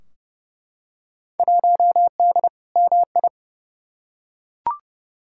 Morse code, heard as a keyed tone, about 31 wpm, 700 Hz. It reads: 1D MI
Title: Field Day Exchanges - 30wpm